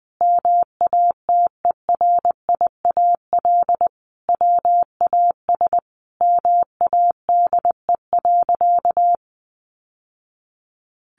Morse code, heard as a keyed tone, about 20 wpm, 700 Hz. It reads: MATERIAL WAS MADE.
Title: Sentences from Top 1200 Words 20wpm